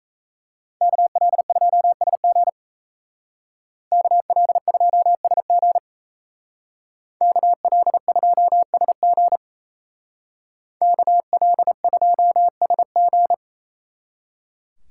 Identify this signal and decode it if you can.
Morse code, heard as a keyed tone, about 35 wpm, 700 Hz. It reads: KL2SG KL2SG KL2SG KL2SG
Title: Speed Racing - USA Call Signs 28wpm